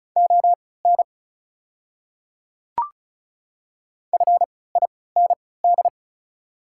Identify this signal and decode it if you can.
Morse code, heard as a keyed tone, about 34 wpm, spaced as for 20 wpm, 690 Hz. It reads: ON FIND
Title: 3x Character Spacing - Top 500 Words - 35wpm